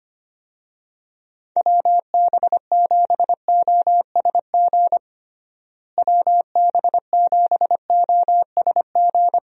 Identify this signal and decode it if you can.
Morse code, heard as a keyed tone, about 25 wpm, 700 Hz. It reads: WB7OSG WB7OSG
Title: USA Call Signs - 6 Characters - Repeated 3x - 25wpm